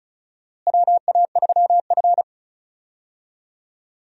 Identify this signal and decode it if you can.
Morse code, heard as a keyed tone, about 36 wpm, 700 Hz. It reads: WA3F
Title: USA Call Signs - 4 Characters - Repeated 3x - 35wpm